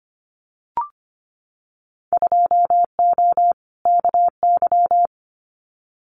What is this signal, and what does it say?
Morse code, heard as a keyed tone, about 25 wpm, 700 Hz. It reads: 2O KY
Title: Winter Field Day Exchanges 25wpm